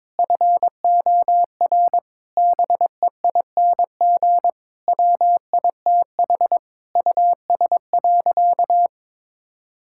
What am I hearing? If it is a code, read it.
Morse code, heard as a keyed tone, about 22 wpm, 700 Hz. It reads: FOR BEING WITH US.
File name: practice / 22wpm / Sentences with 4-Word Phrases - 22wpm